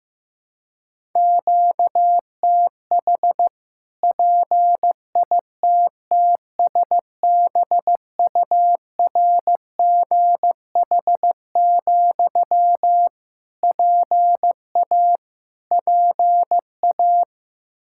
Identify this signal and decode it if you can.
Morse code, heard as a keyed tone, about 15 wpm, 700 Hz. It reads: QTH PITTSBURGH, PA PA
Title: Practice QSO - Exchange 1 - Beginning - 15wpm